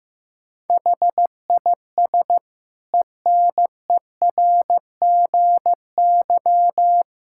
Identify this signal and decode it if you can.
Morse code, heard as a keyed tone, about 15 wpm, 700 Hz. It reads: HIS ENERGY
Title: Sentences from Top 1000 Words - Review Part 6 - 15wpm